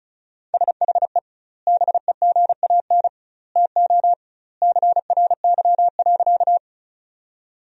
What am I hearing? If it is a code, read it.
Morse code, heard as a keyed tone, about 36 wpm, 700 Hz. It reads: SHE BEGAN TO CRY.
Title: 2x Word-Spacing - Sentences from Top 500 Words - 35wpm